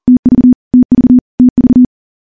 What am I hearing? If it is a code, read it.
Morse code, heard as a keyed tone, about 40 wpm, 250 Hz. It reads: TV TV TV
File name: morse / morse-training-40WPM